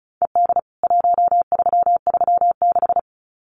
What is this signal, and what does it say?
Morse code, heard as a keyed tone, about 34 wpm, 700 Hz. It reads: ED 1336
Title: CWT Exchange 35wpm